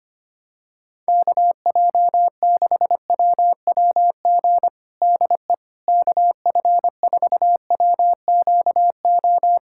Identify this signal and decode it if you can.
Morse code, heard as a keyed tone, about 25 wpm, 700 Hz. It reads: KJ6WWG DE KF4WQO
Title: USA Callsign DE Callsign - 6 Characters - 25wpm